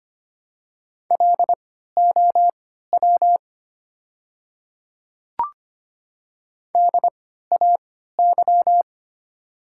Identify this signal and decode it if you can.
Morse code, heard as a keyed tone, about 25 wpm, spaced as for 14 wpm, 690 Hz. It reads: LOW DAY
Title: 3x Character Spacing - Top 200 Words - 25wpm